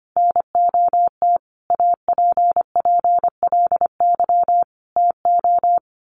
Morse code, heard as a keyed tone, about 25 wpm, 700 Hz. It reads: NOT APPLY TO
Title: Sentences from Top 1200 Words 25wpm